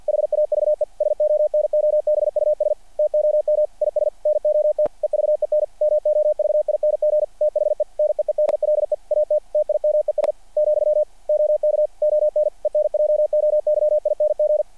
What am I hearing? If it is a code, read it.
Morse code, heard as a keyed tone, about 50 wpm, 600 Hz. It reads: HAVE SONOBUS TOM IS NOT EVEN MOVING THE NEEDLE AT TIMES , OK ON ENJOYING
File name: archived / CRSnet